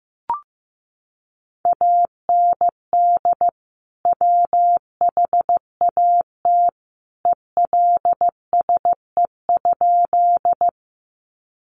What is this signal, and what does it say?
Morse code, heard as a keyed tone, about 15 wpm, 700 Hz. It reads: AND WHAT ELSE?
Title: Sentences from Top 800 Words 15wpm